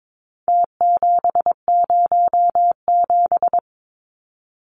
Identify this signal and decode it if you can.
Morse code, heard as a keyed tone, about 22 wpm, 700 Hz. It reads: T707
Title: Numbers - Contest Style - 4-Digit 22wpm